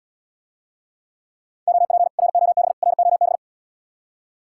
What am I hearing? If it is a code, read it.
Morse code, heard as a keyed tone, about 79 wpm, 700 Hz. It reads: OP ROB ROB